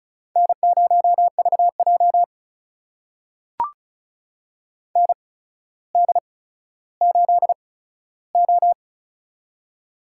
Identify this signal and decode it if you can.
Morse code, heard as a keyed tone, about 36 wpm, spaced as for 15 wpm, 700 Hz. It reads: N0VJ ND8O
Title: USA Call Signs - 4 Characters - 8x Character Spacing - 35wpm